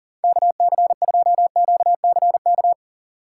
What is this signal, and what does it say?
Morse code, heard as a keyed tone, about 37 wpm, 700 Hz. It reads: KC2QCK
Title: USA Call Signs - 6 Characters - 4x Character Spacing - 40wpm